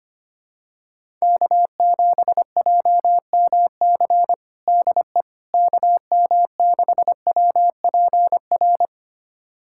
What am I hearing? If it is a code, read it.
Morse code, heard as a keyed tone, about 25 wpm, 700 Hz. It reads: K7JMC DE KM6WPR
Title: USA Callsign DE Callsign 25wpm